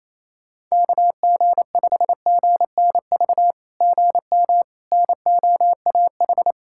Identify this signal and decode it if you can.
Morse code, heard as a keyed tone, about 29 wpm, 700 Hz. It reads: KG5GNV GM NOAH
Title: SOTA QSO Practice 28wpm